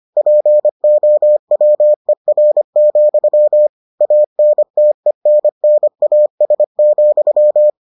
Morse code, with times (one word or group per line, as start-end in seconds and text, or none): 0.2-3.7 POWER,
4.0-7.7 ANTENNAS,